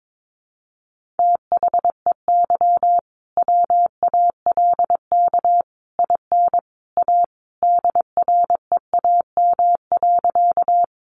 Morse code, heard as a keyed tone, about 22 wpm, 700 Hz. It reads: THEY WALK IN A DREAM.
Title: Sentences from Top 1500 Words 22wpm